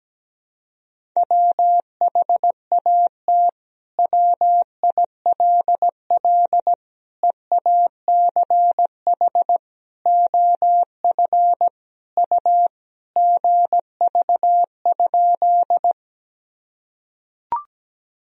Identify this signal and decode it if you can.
Morse code, heard as a keyed tone, about 17 wpm, 700 Hz. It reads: WHAT WILL EACH OF U GV?
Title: Sentences from Top 200 Words and Abbreviations 17wpm